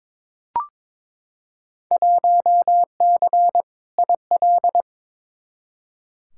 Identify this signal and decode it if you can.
Morse code, heard as a keyed tone, about 22 wpm, 700 Hz. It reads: 1C IL
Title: Field Day Exchanges - 22wpm